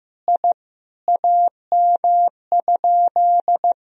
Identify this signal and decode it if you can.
Morse code, heard as a keyed tone, about 15 wpm, 700 Hz. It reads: I AM?
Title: Sentences from Top 1000 Words - Review Part 2 - 15wpm